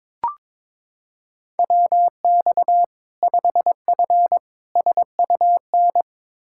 Morse code, heard as a keyed tone, about 22 wpm, 700 Hz. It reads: WX 5F SUN
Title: Practice QSO - Exchange 2 - Info - 22wpm